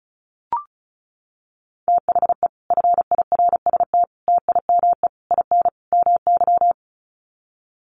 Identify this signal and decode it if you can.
Morse code, heard as a keyed tone, about 34 wpm, 700 Hz. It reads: THE FIRST TIME IN MY
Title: Sets of 5 Words Encore 35wpm